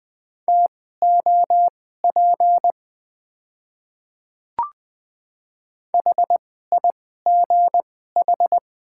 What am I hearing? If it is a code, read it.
Morse code, heard as a keyed tone, about 20 wpm, spaced as for 14 wpm, 700 Hz. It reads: TOP HIGH
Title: 2x Character Spacing - Top 500 Words - 20wpm